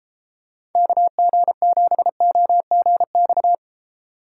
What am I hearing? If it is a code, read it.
Morse code, heard as a keyed tone, about 32 wpm, 700 Hz. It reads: KG7OGX